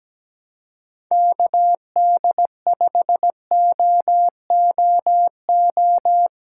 Morse code, heard as a keyed tone, about 17 wpm, 700 Hz. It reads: KD5OOO
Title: USA Call Signs - 6 Characters - 4x Character Spacing - 17wpm